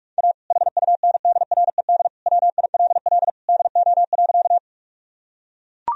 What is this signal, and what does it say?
Morse code, heard as a keyed tone, about 42 wpm, 700 Hz. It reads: A HUNDRED WILL DO.
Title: Sentences from Top 1000 Words - Review Part 2 - 45wpm